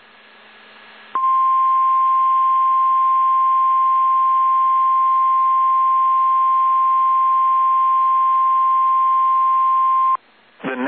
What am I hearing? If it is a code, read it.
Emergency Alert System (SAME) tones (the 1050 Hz attention tone). It is heard from 1.1 to 10.2 s.